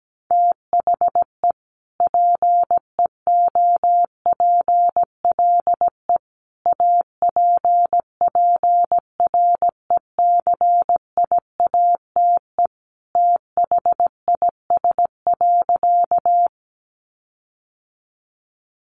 Morse code, heard as a keyed tone, about 17 wpm, 700 Hz. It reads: THE PEOPLE APPRECIATE THIS.
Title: Sentences from Top 1300 Words 17wpm